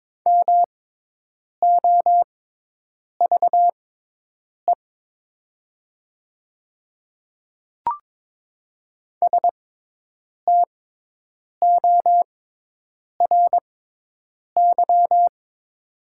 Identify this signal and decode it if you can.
Morse code, heard as a keyed tone, about 22 wpm, spaced as for 8 wpm, 700 Hz. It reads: MOVE STORY
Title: 6x Character Spacing - Top 300 Words - 22wpm